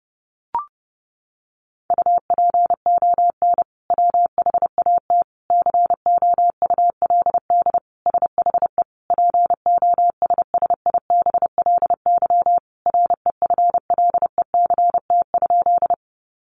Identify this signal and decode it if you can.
Morse code, heard as a keyed tone, about 30 wpm, 700 Hz. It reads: UPON WHAT COULD SHE POSSIBLY REFLECT?
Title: Sentences from Top 1500 Words 30wpm